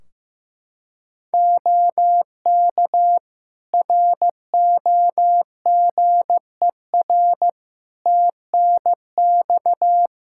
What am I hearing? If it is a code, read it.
Morse code, heard as a keyed tone, about 15 wpm, 700 Hz. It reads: OK ROGER TNX
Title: Practice QSO - Exchange 4 - Ending - 15wpm